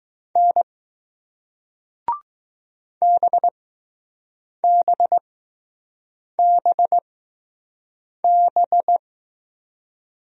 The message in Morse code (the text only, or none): N BBBB